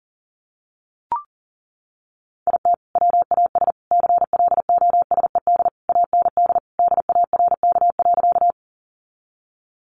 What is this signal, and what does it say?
Morse code, heard as a keyed tone, about 40 wpm, 700 Hz. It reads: IT WAS CLOSED AND DARK.
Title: Sentences from Top 1400 Words 40wpm